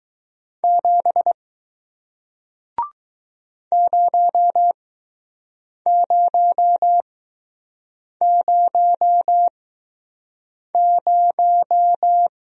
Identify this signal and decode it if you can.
Morse code, heard as a keyed tone, about 20 wpm, spaced as for 6 wpm, 700 Hz. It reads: 7 0000